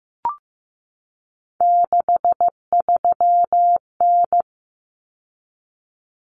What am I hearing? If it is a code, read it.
Morse code, heard as a keyed tone, about 15 wpm, 700 Hz. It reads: 63N
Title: Common Cut Numbers - 3-Digit 15wpm